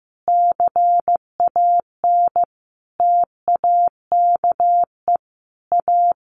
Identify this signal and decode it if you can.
Morse code, heard as a keyed tone, about 15 wpm, 700 Hz. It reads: CAN TAKE A
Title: A Sound Of Thunder 15wpm